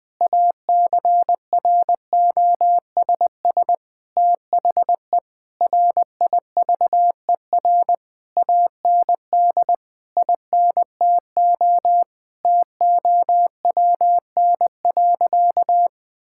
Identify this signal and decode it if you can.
Morse code, heard as a keyed tone, about 20 wpm, 700 Hz. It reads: ACROSS THE RIVER AND INTO TOWN.
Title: Sentences from Top 1000 Words - Review Part 6 - 20wpm